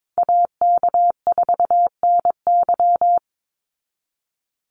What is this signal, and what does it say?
Morse code, heard as a keyed tone, about 22 wpm, 700 Hz. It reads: AK4NY
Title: Call Signs Repeated 2x - 22wpm